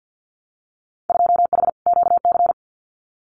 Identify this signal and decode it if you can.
Morse code, heard as a keyed tone, about 48 wpm, 700 Hz. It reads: 2H QC
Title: Winter Field Day Exchanges 50wpm